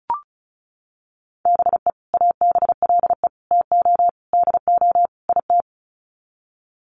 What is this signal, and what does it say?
Morse code, heard as a keyed tone, about 34 wpm, 700 Hz. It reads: BE ABLE TO DO IT
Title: Sets of 5 Words Encore2 35wpm